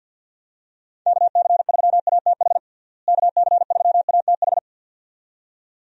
Morse code, heard as a keyed tone, about 50 wpm, 700 Hz. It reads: KC3ATH KC3ATH
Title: USA Call Signs - 6 Characters - Repeated 2x - 50wpm